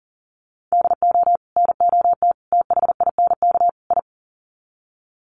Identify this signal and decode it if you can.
Morse code, heard as a keyed tone, about 40 wpm, 700 Hz. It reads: DO NOT THINK I
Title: Sets of 4 Words 40wpm